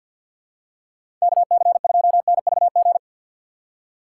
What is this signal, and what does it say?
Morse code, heard as a keyed tone, about 52 wpm, 700 Hz. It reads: KC2NVG
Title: USA Call Signs - 6 Characters - 6x Character Spacing - 50wpm